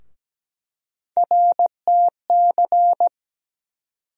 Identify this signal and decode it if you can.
Morse code, heard as a keyed tone, about 17 wpm, 700 Hz. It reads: RTC
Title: USA Call Sign Postfixes - 3 Letters 17wpm